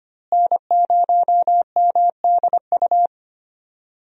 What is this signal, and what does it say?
Morse code, heard as a keyed tone, about 25 wpm, 700 Hz. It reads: N0MDU
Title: USA Call Signs - 5 Characters - Repeated 3x - 25wpm